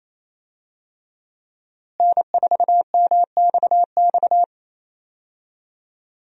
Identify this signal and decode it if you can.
Morse code, heard as a keyed tone, about 28 wpm, 700 Hz. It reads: N4MXX
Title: USA Call Signs - 5 Characters - Repeated 3x - 28wpm